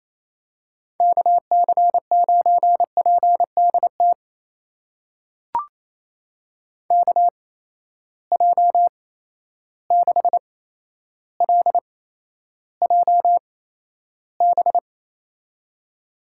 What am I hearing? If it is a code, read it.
Morse code, heard as a keyed tone, about 28 wpm, spaced as for 12 wpm, 700 Hz. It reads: KC9PDT KJ6LJB
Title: USA Call Signs - 6 Characters - 8x Character Spacing - 28wpm